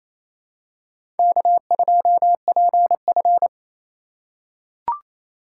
Morse code, heard as a keyed tone, about 28 wpm, 700 Hz. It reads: K2PF
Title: USA Call Signs - 4 Characters - 4x Character Spacing - 28wpm